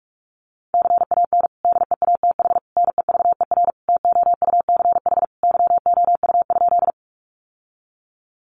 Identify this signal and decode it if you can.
Morse code, heard as a keyed tone, about 44 wpm, 700 Hz. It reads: CAN DEATH NEVER TOUCH YOU?
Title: Sentences from Top 1200 Words 45wpm